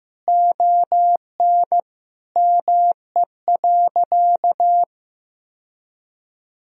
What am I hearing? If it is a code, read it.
Morse code, heard as a keyed tone, about 15 wpm, 700 Hz. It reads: ON ME.
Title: Sentences from Top 1000 Words - Review Part 2 - 15wpm